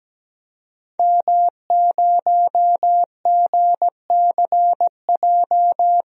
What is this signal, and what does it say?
Morse code, heard as a keyed tone, about 17 wpm, 700 Hz. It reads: M0GCJ
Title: DXCC-100 Call Signs - 5 Characters - Repeated 2x - 17wpm